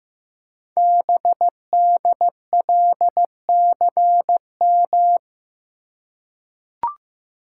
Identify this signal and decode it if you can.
Morse code, heard as a keyed tone, about 15 wpm, 700 Hz. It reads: BDLCM
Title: Code Groups - 5 Letters - 15wpm